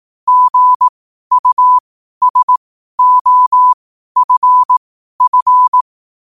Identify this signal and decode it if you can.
Morse code, heard as a keyed tone, about 18 wpm, spaced as for 12 wpm, 1000 Hz. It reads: GUSOFF
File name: qotdfiles / 12wpm